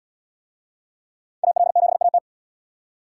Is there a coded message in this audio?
Yes: Morse code from 1.4 to 2.2 s, heard as a keyed tone, about 75 wpm, 700 Hz. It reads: AL7AT